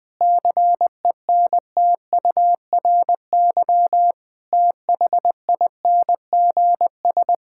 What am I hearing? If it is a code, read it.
Morse code, heard as a keyed tone, about 21 wpm, 700 Hz. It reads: CENTURY THINGS
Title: Sentences with 5-Word Phrases - Spoken Beginning - 20wpm